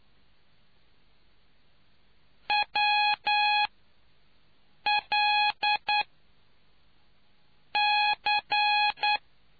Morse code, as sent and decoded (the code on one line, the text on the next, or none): .-- .-.. -.-.
WLC